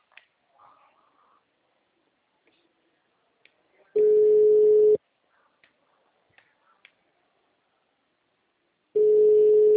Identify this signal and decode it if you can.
Ringback tone (Europe, 425 Hz).
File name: out- -2014- - - .92